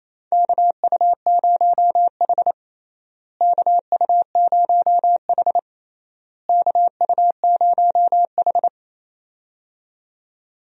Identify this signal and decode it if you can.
Morse code, heard as a keyed tone, about 28 wpm, 700 Hz. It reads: KU0H KU0H KU0H
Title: USA Call Signs - 4 Characters - Repeated 3x - 28wpm